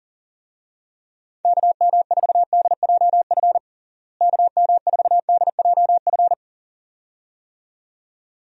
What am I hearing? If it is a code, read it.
Morse code, heard as a keyed tone, about 40 wpm, 700 Hz. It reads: KM4DJF KM4DJF
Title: USA Call Signs - 6 Characters - Repeated 2x - 40wpm